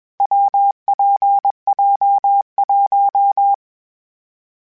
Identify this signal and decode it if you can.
Morse code, heard as a keyed tone, about 21 wpm, 800 Hz. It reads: WPJ1
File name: Variable Speed / Variable Speed-mp3s / WPJ WPJ1 sp @20